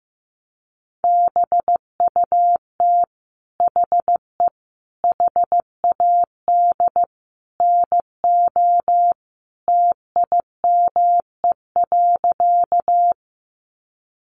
Morse code, heard as a keyed tone, about 15 wpm, 700 Hz. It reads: BUT HE HAD NO TIME.
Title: Sentences from Top 100 Words 15wpm